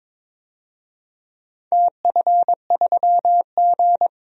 Morse code, heard as a keyed tone, about 22 wpm, 700 Hz. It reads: TF3G
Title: DXCC-100 Call Signs - 4 Characters - Repeated 2x - 22wpm